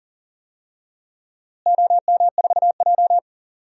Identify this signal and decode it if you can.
Morse code, heard as a keyed tone, about 39 wpm, 700 Hz. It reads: OM4J
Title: DXCC-100 Call Signs - 4 Characters - Repeated 3x - 40wpm